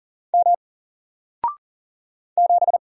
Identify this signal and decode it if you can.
Morse code, heard as a keyed tone, about 40 wpm, 700 Hz. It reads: M 7
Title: Single Letter-Number Alt 40wpm